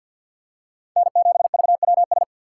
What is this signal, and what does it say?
Morse code, heard as a keyed tone, about 47 wpm, 700 Hz. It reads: N7VWS
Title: USA Call Signs - 5 Characters - 6x Character Spacing - 50wpm